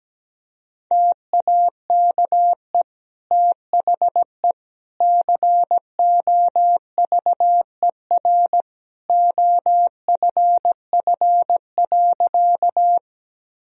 Morse code, heard as a keyed tone, about 17 wpm, 700 Hz. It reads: TAKE THE COVER OFF.
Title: Sentences from Top 1000 Words - Review Part 1 - 17wpm